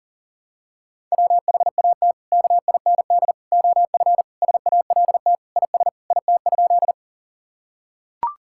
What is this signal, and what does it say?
Morse code, heard as a keyed tone, about 40 wpm, 700 Hz. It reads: WHAT KIND OF SALT IS IT?
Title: Sentences from Top 1000 Words - Review Part 5 - 40wpm